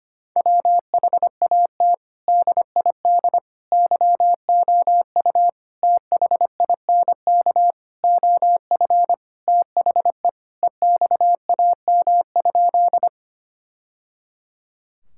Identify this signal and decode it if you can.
Morse code, heard as a keyed tone, about 25 wpm, 700 Hz. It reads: WHAT DID YOU THINK OF THE EXAM?
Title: Sentences from Top 1000 Words - Review Part 5 - 25wpm